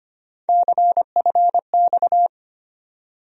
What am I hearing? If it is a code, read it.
Morse code, heard as a keyed tone, about 25 wpm, 700 Hz. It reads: CFX
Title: USA Call Sign Postfixes - 3 Letters 25wpm